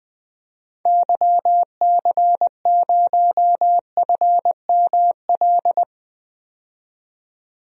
Morse code, heard as a keyed tone, about 20 wpm, 700 Hz. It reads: YC0FML
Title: DXCC-100 Call Signs - 6 Characters - 20wpm